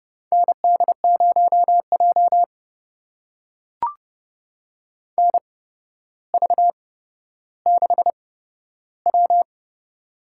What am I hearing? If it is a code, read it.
Morse code, heard as a keyed tone, about 30 wpm, spaced as for 13 wpm, 700 Hz. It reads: ND0J NV6W
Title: USA Call Signs - 4 Characters - 8x Character Spacing - 30wpm